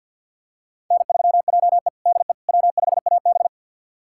Morse code, heard as a keyed tone, about 51 wpm, 700 Hz. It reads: N3JE DE W5AB
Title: USA Callsign DE Callsign - 4 Characters - 50wpm